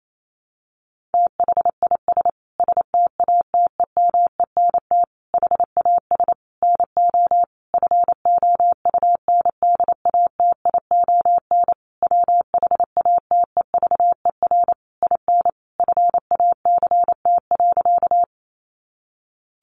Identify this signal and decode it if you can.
Morse code, heard as a keyed tone, about 28 wpm, 700 Hz. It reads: THIS STATEMENT HAS NO FOUNDATION WHATEVER IN FACT.
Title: Sentences from Top 1200 Words 28wpm